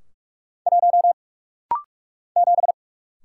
Morse code, heard as a keyed tone, about 45 wpm, 700 Hz. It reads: 1 7
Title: Single Letter-Number - Rapid-Fire 45wpm